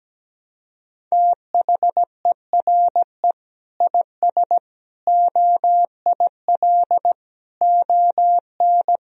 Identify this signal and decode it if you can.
Morse code, heard as a keyed tone, about 17 wpm, 700 Hz. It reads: THERE IS OIL ON
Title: Sentences from Top 1000 Words - Review Part 4 - 17wpm